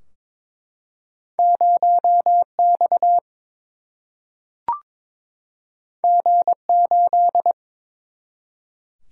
Morse code, heard as a keyed tone, about 22 wpm, 700 Hz. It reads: 0 X G 8
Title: Single-Letter Single-Number 22wpm